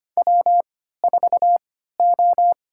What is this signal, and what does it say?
Morse code, heard as a keyed tone, about 25 wpm, spaced as for 14 wpm, 700 Hz. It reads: W4O
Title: USA Call Signs - 3x Character Spacing - 25wpm